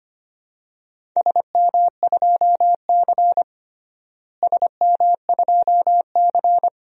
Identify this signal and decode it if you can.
Morse code, heard as a keyed tone, about 25 wpm, 700 Hz. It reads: SM2C SM2C
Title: DXCC-100 Call Signs - 4 Characters - Repeated 2x - 25wpm